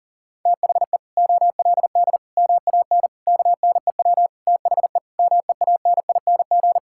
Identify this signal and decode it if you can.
Morse code, heard as a keyed tone, about 40 wpm, 700 Hz. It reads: THE OLD MAN KNEW THE MEANING
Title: Sentences from Top 1000 Words - Review Part 6 - 40wpm